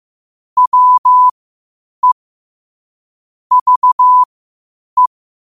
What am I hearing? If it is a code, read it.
Morse code, heard as a keyed tone, about 15 wpm, spaced as for 8 wpm, 1000 Hz. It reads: WE VE
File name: qotdfiles / 8wpm